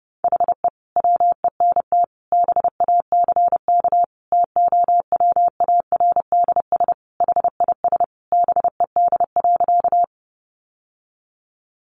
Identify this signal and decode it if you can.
Morse code, heard as a keyed tone, about 28 wpm, 700 Hz. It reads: HE WENT BACK TOWARDS HIS BED.
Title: Sentences from Top 1100 Words 30wpm